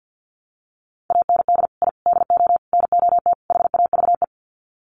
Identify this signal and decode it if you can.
Morse code, heard as a keyed tone, about 50 wpm, 700 Hz. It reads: AND I DO NOT HAVE
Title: Sets of 5 Words Encore 50wpm